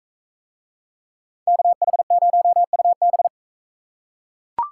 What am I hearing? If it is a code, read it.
Morse code, heard as a keyed tone, about 42 wpm, 700 Hz. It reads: KH0UB